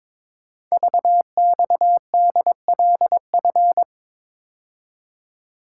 Morse code, heard as a keyed tone, about 23 wpm, 700 Hz. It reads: VXDLF
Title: Code Groups - 5 Letters - 22wpm